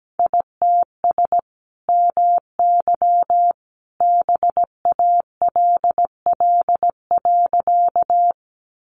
Morse code, heard as a keyed tone, about 17 wpm, 700 Hz. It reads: ITS MY BALL.
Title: Sentences from Top 900 Words 17wpm